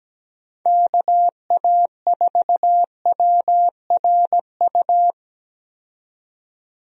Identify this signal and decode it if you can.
Morse code, heard as a keyed tone, about 17 wpm, 700 Hz. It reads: KA4WRU
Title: USA Call Signs - 6 Characters - 17wpm